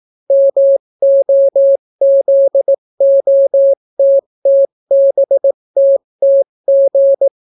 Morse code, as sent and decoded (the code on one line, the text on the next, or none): -- --- --.. --- - - -... - - --.
MOZOTTBTTG